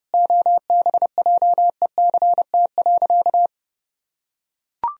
Morse code, heard as a keyed tone, about 31 wpm, 700 Hz. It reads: OBJECT.
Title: Sentences from Top 1000 Words - Review Part 3 - 30wpm